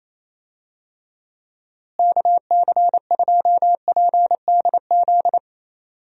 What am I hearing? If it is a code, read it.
Morse code, heard as a keyed tone, about 28 wpm, 700 Hz. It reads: KC2PDZ
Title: USA Call Signs - 6 Characters - Repeated 3x - 28wpm